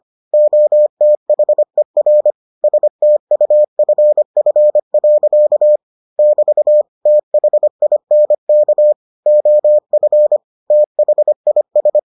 Morse code, heard as a keyed tone, about 25 wpm, 600 Hz. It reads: OTHER STUFF. <BT> THINK OF THIS